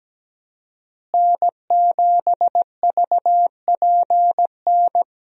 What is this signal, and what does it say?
Morse code, heard as a keyed tone, about 17 wpm, 700 Hz. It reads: N7VPN
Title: USA Call Signs - 5 Characters - Repeated 2x - 17wpm